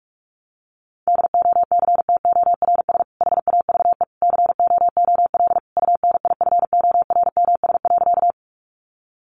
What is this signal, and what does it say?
Morse code, heard as a keyed tone, about 44 wpm, 700 Hz. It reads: DOCTORS HAVE COOL UNIFORMS.
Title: Easy Phrases 45wpm